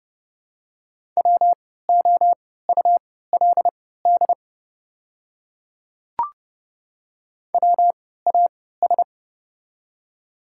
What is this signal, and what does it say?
Morse code, heard as a keyed tone, about 30 wpm, spaced as for 17 wpm, 700 Hz. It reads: WOULD WAS
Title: 3x Character Spacing - Top 100 Words - 30wpm